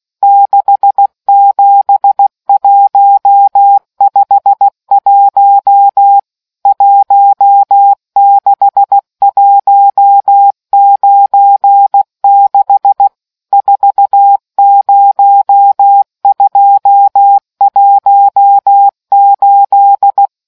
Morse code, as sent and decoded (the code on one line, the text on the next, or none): -.... --... .---- ..... .---- / .---- -.... .---- ----. -.... / ....- ----- ..--- .---- ---..
67151 16196 40218